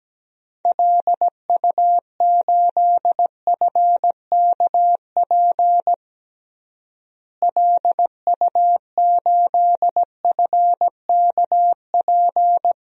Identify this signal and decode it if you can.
Morse code, heard as a keyed tone, about 17 wpm, 700 Hz. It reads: LU8FKP LU8FKP
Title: DXCC-100 Call Signs - 6 Characters - Repeated 3x - 17wpm